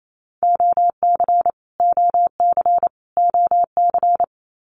Morse code, heard as a keyed tone, about 28 wpm, 700 Hz. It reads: OC OC OC
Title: Abbreviations Repeated 3x - 28wpm